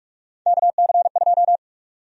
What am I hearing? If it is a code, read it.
Morse code, heard as a keyed tone, about 45 wpm, 700 Hz. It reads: KC2
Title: USA Call Sign Prefixes - 45wpm